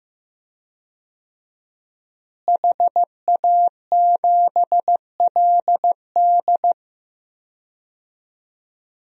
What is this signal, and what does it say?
Morse code, heard as a keyed tone, about 14 wpm, 700 Hz. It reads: HA7LD
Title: DXCC-100 Call Signs - 5 Characters - Repeated 3x - 15wpm